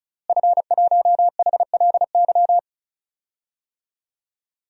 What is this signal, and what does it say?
Morse code, heard as a keyed tone, about 36 wpm, 700 Hz. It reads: F1HLY
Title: DXCC-100 Call Signs - 5 Characters - Repeated 3x - 35wpm